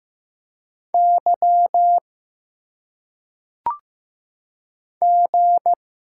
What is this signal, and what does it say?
Morse code, heard as a keyed tone, about 15 wpm, 700 Hz. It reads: Y G
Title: Single Letters - Phonetic Pronunciation 15wpm